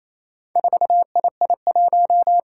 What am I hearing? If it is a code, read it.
Morse code, heard as a keyed tone, about 28 wpm, 700 Hz. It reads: 4II1
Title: Four Challenging Letter-Number Combinations 28wpm